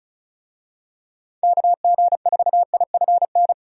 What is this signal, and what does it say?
Morse code, heard as a keyed tone, about 36 wpm, 700 Hz. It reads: KG4IFN
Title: USA Call Signs - 6 Characters - 35wpm